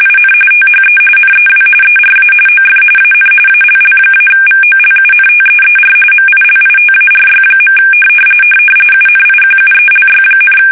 RTTY, about 45.45 baud. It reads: IN THE NEW ERA OF AN ALL-VOLUNTEER MILITARY. UNIQUELY,
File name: fsk850 / battleshipnj